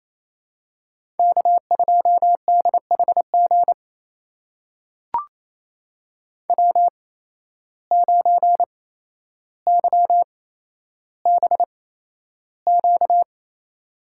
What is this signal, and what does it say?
Morse code, heard as a keyed tone, about 28 wpm, spaced as for 12 wpm, 700 Hz. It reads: K2DHG W9YBQ
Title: USA Call Signs - 5 Characters - 8x Character Spacing - 28wpm